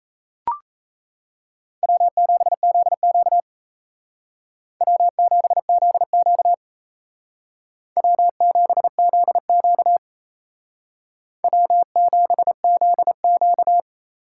Morse code, heard as a keyed tone, about 36 wpm, 700 Hz. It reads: W7ZQ W7ZQ W7ZQ W7ZQ